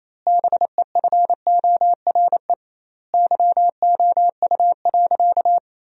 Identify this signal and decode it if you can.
Morse code, heard as a keyed tone, about 27 wpm, 700 Hz. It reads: BEFORE YOU.
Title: 2x Word-Spacing - Sentences from Top 200 Words - 28wpm